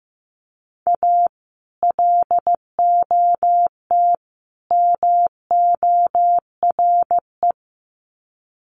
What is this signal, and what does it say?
Morse code, heard as a keyed tone, about 15 wpm, 700 Hz. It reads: A LOT MORE
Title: Sets of 3 Words 15wpm